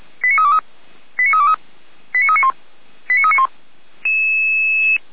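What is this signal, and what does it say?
Five-tone selective call (ZVEI), heard as quick runs of five tones, tones of about 70 ms. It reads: 89323 89323 89391 89391
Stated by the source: ZVEI five-tone call received by SDR.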